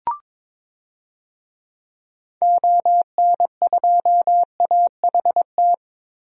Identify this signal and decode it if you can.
Morse code, heard as a keyed tone, about 22 wpm, 700 Hz. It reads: ON2AHT
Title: DXCC-100 Call Signs - 6 Characters - 22wpm